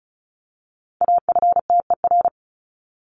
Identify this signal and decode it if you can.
Morse code, heard as a keyed tone, about 34 wpm, 700 Hz. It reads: AFTER
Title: Top 300 Words Repeated 35wpm